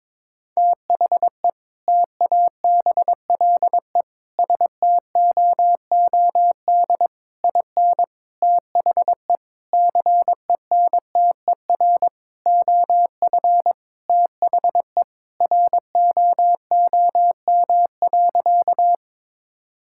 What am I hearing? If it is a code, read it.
Morse code, heard as a keyed tone, about 22 wpm, 700 Hz. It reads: THE TABLE STOOD IN THE CENTER OF THE ROOM.
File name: practice / 22wpm / Sentences with 4-Word Phrases - 22wpm